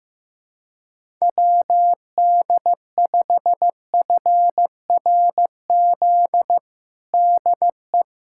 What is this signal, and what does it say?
Morse code, heard as a keyed tone, about 16 wpm, 700 Hz. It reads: WD5FRZ DE
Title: USA Callsign DE Callsign 15wpm